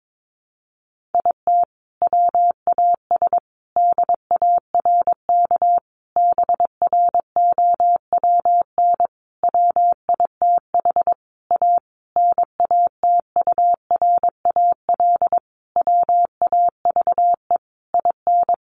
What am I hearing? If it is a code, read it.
Morse code, heard as a keyed tone, about 22 wpm, 700 Hz. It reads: IT WAS DARK BROWN WITH A NATURAL WAVE IN
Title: Sentences from Top 1400 Words 22wpm